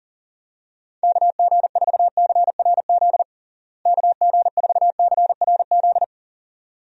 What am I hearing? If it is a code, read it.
Morse code, heard as a keyed tone, about 40 wpm, 700 Hz. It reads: KG4CRZ KG4CRZ
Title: USA Call Signs - 6 Characters - Repeated 2x - 40wpm